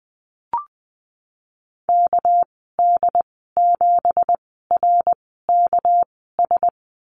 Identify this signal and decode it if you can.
Morse code, heard as a keyed tone, about 20 wpm, 700 Hz. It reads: K D 7 R K S
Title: 238 Lesson 38g - Letter q - Call Signs 20wpm